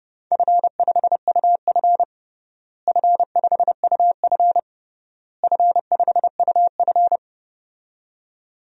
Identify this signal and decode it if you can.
Morse code, heard as a keyed tone, about 32 wpm, 700 Hz. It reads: F5UF F5UF F5UF
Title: DXCC-100 Call Signs - 4 Characters - Repeated 3x - 30wpm